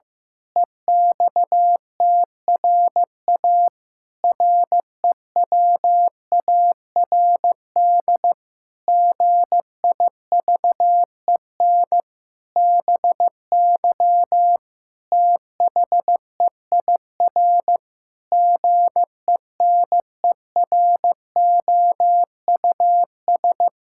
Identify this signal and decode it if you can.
Morse code, heard as a keyed tone, about 15 wpm, 700 Hz. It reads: EXTRA REWARD GIVEN BY THEIR GENEROUS
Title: Pangram Challenge v2 - 15wpm